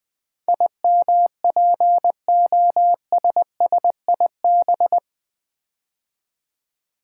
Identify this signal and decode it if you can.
Morse code, heard as a keyed tone, about 20 wpm, 700 Hz. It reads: IMPOSSIB
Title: Build A Word - 4 Letters At A Time - 10-Letter Words - 20wpm